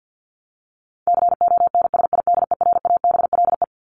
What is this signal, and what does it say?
Morse code, heard as a keyed tone, about 50 wpm, 700 Hz. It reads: CONSIDERABLE
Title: LY Suffix - Top 100 Words with Base Words 50wpm